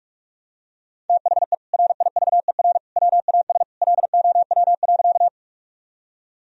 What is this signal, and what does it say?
Morse code, heard as a keyed tone, about 46 wpm, 700 Hz. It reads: THE RIVER WAS LOW.
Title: Sentences from Top 1000 Words - Review Part 1 - 45wpm